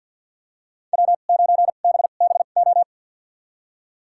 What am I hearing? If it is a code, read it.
Morse code, heard as a keyed tone, about 47 wpm, spaced as for 35 wpm, 700 Hz. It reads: W9BBO
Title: USA Call Signs - 5 Characters - 2x Character Spacing - 50wpm